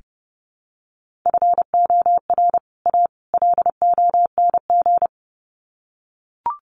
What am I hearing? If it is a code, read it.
Morse code, heard as a keyed tone, about 30 wpm, 700 Hz. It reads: FOR A LONG
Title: Sets of 3 Words 30wpm